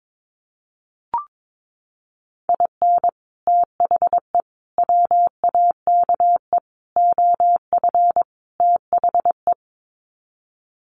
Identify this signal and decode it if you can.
Morse code, heard as a keyed tone, about 22 wpm, 700 Hz. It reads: IN THE WAKE OF THE
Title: Sets of 5 Words 22wpm